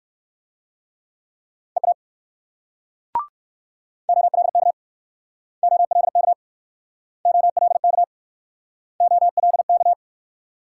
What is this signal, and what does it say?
Morse code, heard as a keyed tone, about 57 wpm, 690 Hz. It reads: EA OLK OLK OLK OLK